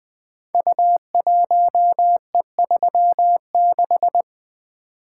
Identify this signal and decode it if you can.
Morse code, heard as a keyed tone, about 20 wpm, 700 Hz. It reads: U1E36
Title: Code Groups - 5 Letter:Numbers - 20wpm